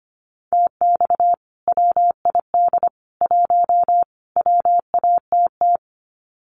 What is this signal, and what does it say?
Morse code, heard as a keyed tone, about 24 wpm, 700 Hz. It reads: TX WID 1 WATT
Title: Common QSO Phrases 25wpm